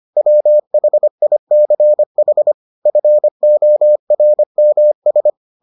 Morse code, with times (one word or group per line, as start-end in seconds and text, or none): 0.2-2.5 WHICH
2.9-5.3 FORMS